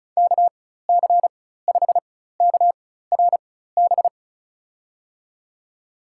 Morse code, heard as a keyed tone, about 36 wpm, spaced as for 16 wpm, 700 Hz. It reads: KC5KRB
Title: USA Call Signs - 6 Characters - 4x Character Spacing - 35wpm